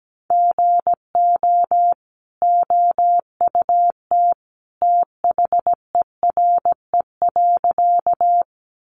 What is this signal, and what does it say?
Morse code, heard as a keyed tone, about 17 wpm, 700 Hz. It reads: GO OUT THERE.
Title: A Sound Of Thunder 17wpm